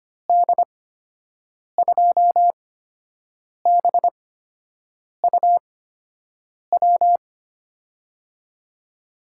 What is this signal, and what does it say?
Morse code, heard as a keyed tone, about 25 wpm, spaced as for 7 wpm, 700 Hz. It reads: D2BUW
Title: USA Call Signs - 6 Characters - 8x Character Spacing - 25wpm